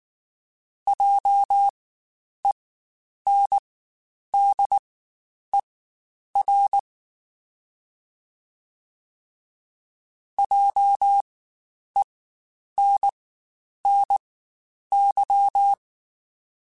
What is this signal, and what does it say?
Morse code, heard as a keyed tone, about 19 wpm, spaced as for 9 wpm, 800 Hz. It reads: JENDER JENNY
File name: Spaced Alphas / Spaced Alphas mp3s / Js sp @18